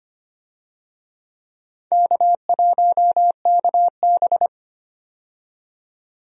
Morse code, heard as a keyed tone, about 25 wpm, 700 Hz. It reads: K1KB
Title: USA Call Signs - 4 Characters - 25wpm